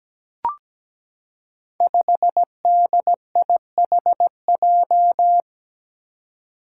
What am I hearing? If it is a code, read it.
Morse code, heard as a keyed tone, about 17 wpm, 700 Hz. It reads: 5DIHJ
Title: Five Challenging Letter-Number Combinations 17wpm